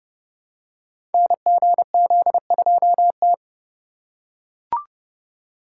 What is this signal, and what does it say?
Morse code, heard as a keyed tone, about 31 wpm, 700 Hz. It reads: NGZ2T
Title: Code Groups - 5 Letter:Numbers - 30wpm